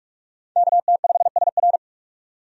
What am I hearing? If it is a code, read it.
Morse code, heard as a keyed tone, about 46 wpm, 700 Hz. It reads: KT5SR
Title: USA Call Signs - 5 Characters - 45wpm